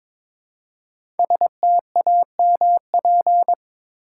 Morse code, heard as a keyed tone, about 22 wpm, 700 Hz. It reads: STAMP
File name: 5 Letter Words - First 3 Letters Given - 22wpm